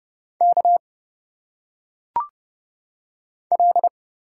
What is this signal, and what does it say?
Morse code, heard as a keyed tone, about 30 wpm, 700 Hz. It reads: K L
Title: Single Letters - Phonetic Pronunciation 30wpm